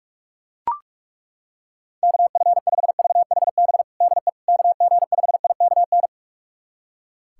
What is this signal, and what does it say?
Morse code, heard as a keyed tone, about 41 wpm, 700 Hz. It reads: KF5VHB DE KG5IKN
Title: USA Callsign DE Callsign - 6 Characters - 45wpm